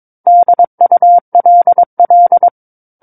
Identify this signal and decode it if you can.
Morse code, heard as a keyed tone, about 22 wpm, 700 Hz. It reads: DULL
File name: Book One / 13-Chapter 13 - 22 WPM